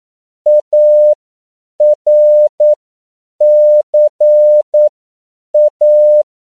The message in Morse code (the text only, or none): ARCA